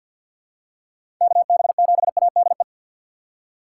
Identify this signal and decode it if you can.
Morse code, heard as a keyed tone, about 51 wpm, 700 Hz. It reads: KB7ADE
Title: USA Call Signs - 6 Characters - 4x Character Spacing - 50wpm